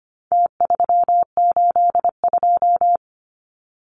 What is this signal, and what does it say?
Morse code, heard as a keyed tone, about 25 wpm, 700 Hz. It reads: T382
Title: Numbers - Contest Style - 4-Digit 25wpm